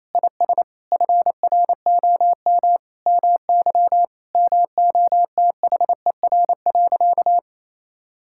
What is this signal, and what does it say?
Morse code, heard as a keyed tone, about 27 wpm, 700 Hz. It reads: IS FROM MY MOTHER.
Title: Sentences from Top 1000 Words - Review Part 6 - 28wpm